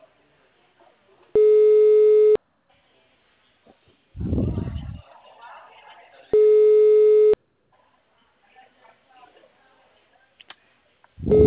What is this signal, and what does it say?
Ringback tone (Europe, 425 Hz), with speech.